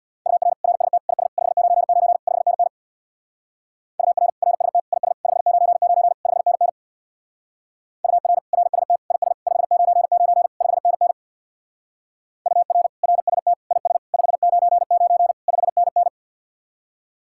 Morse code, heard as a keyed tone, about 61 wpm, 700 Hz. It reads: UR RST IS 599 5NN UR RST IS 599 5NN UR RST IS 599 5NN UR RST IS 599 5NN
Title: Speed-Racing Common QSO Phrases - 50wpm